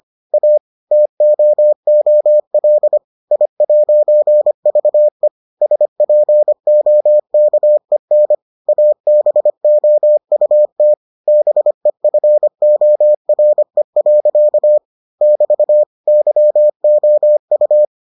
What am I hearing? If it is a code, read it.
Morse code, heard as a keyed tone, about 25 wpm, 600 Hz. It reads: A TOOL I'VE SPOKEN ABOUT BEFORE. <BT> YOU